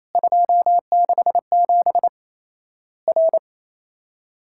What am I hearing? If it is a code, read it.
Morse code, heard as a keyed tone, about 28 wpm, 700 Hz. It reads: 267 R
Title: Worked All Europe DX Contest - QTC Exchange - 28wpm